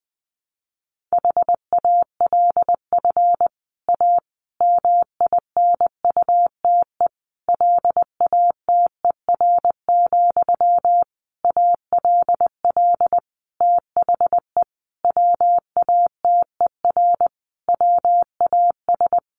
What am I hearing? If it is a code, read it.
Morse code, heard as a keyed tone, about 20 wpm, 700 Hz. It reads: HALF A MINUTE LATER, ALL THE WATER WAS
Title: Sentences from Top 1100 Words 20wpm